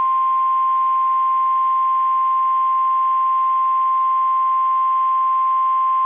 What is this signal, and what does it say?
Emergency Alert System (SAME) tones (the 1050 Hz attention tone). It is heard from 0.8 to 5.8 s.
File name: eas / KSDZKDJL / KSDZKDJL